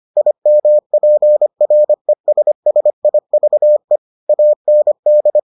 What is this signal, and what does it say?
Morse code, heard as a keyed tone, about 25 wpm, 600 Hz. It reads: IMPRESSIVE AND